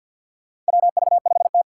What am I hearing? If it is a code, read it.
Morse code, heard as a keyed tone, about 51 wpm, 700 Hz. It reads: WV5T
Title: Four Challenging Letter-Number Combinations 50wpm